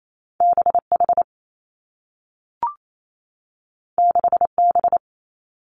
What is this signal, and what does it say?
Morse code, heard as a keyed tone, about 28 wpm, 700 Hz. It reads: B H 6 B
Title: Two Challenging Letter-Number Combinations 28wpm